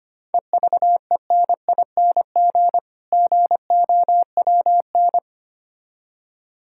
Morse code, heard as a keyed tone, about 25 wpm, 700 Hz. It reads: EVENING GOWN
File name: practice / 25wpm / Difficult Words with Hints - 25wpm